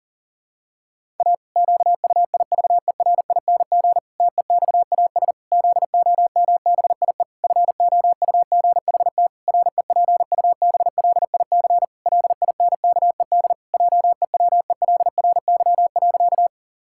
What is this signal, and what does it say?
Morse code, heard as a keyed tone, about 40 wpm, 700 Hz. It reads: A QUIVERING TEXAS ZOMBIE FOUGHT REPUBLIC LINKED JEWELRY.
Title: Pangram Challenge v2 - 40wpm